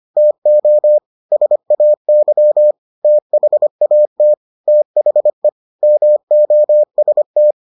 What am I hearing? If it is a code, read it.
Morse code, heard as a keyed tone, about 25 wpm, 600 Hz. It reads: TO SAY THAT THE MOST